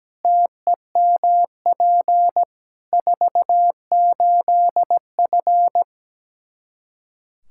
Morse code, heard as a keyed tone, about 17 wpm, 700 Hz. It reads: TEMP 48F
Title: K1USN SST QSO Practice 17wpm